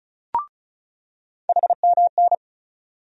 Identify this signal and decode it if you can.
Morse code, heard as a keyed tone, about 34 wpm, 700 Hz. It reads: HMN
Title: USA Call Sign Postfixes - 3 Letters 35wpm